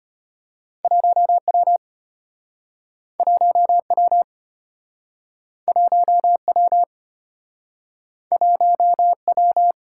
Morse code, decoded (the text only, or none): 1W 1W 1W 1W